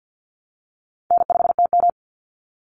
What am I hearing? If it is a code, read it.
Morse code, heard as a keyed tone, about 48 wpm, 700 Hz. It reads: N5TM
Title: Call Signs Repeated 3x - 50wpm